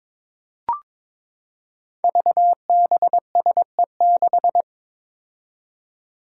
Morse code, heard as a keyed tone, about 22 wpm, 700 Hz. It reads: VBSE6
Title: Five Challenging Letter-Number Combinations 22wpm